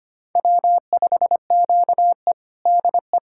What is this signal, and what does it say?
Morse code, heard as a keyed tone, about 25 wpm, 700 Hz. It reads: W5QE DE
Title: USA Callsign DE Callsign - 4 Characters - 25wpm